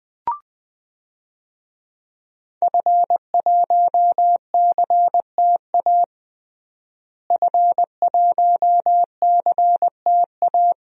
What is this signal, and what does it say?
Morse code, heard as a keyed tone, about 20 wpm, 700 Hz. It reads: F1CTA F1CTA
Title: DXCC-100 Call Signs - 5 Characters - Repeated 2x - 20wpm